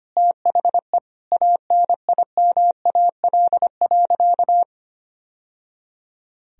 Morse code, heard as a keyed tone, about 25 wpm, 700 Hz. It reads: THE ANIMAL.
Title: Sentences from Top 1000 Words - Review Part 1 - 25wpm